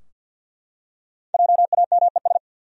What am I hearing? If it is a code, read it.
Morse code, heard as a keyed tone, about 52 wpm, 700 Hz. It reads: JAMES
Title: USA Common Names - 50wpm